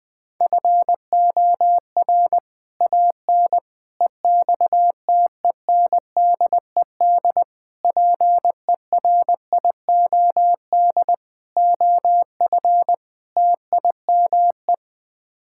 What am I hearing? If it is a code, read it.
Morse code, heard as a keyed tone, about 20 wpm, 700 Hz. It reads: FOR AN EXTENDED PERIOD OF TIME
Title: Sets of 6 Words - 20wpm